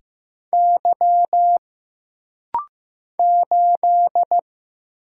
Morse code, heard as a keyed tone, about 15 wpm, 700 Hz. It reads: Y 8
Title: Single Letter-Number Alt - Rapid-Fire 15wpm